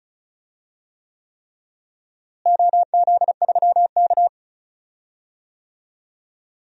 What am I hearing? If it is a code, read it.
Morse code, heard as a keyed tone, about 36 wpm, 700 Hz. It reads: OZ3K
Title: DXCC-100 Call Signs - 4 Characters - Repeated 3x - 35wpm